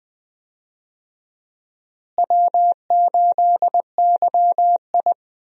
Morse code, heard as a keyed tone, about 20 wpm, 700 Hz. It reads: W8YI
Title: USA Call Signs - 4 Characters - Repeated 3x - 20wpm